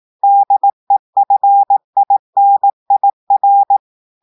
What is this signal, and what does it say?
Morse code, heard as a keyed tone, about 18 wpm, 800 Hz. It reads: DEFINIR
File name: audios / portugais / portugais texte en clair 2 vitesse 18 mots minutes